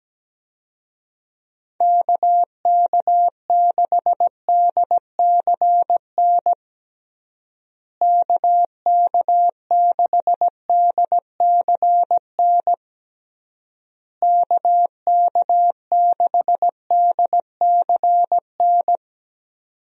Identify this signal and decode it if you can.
Morse code, heard as a keyed tone, about 17 wpm, 700 Hz. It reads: KK6DCN KK6DCN KK6DCN
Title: USA Call Signs - 6 Characters - Repeated 3x - 17wpm